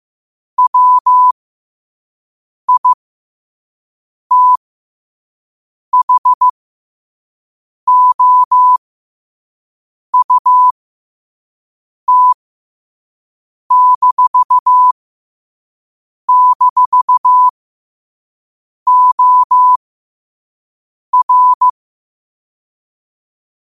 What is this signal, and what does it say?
Morse code, heard as a keyed tone, about 15 wpm, spaced as for 5 wpm, 1000 Hz. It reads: WITHOUT--OR
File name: 5wpm